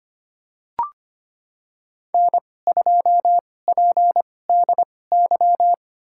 Morse code, heard as a keyed tone, about 25 wpm, spaced as for 18 wpm, 700 Hz. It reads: N2PDY
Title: USA Call Signs - 5 Characters - 2x Character Spacing - 25wpm